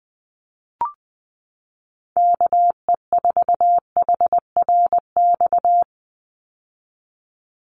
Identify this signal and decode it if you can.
Morse code, heard as a keyed tone, about 20 wpm, 700 Hz. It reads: KE4HRX
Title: USA Call Signs 20wpm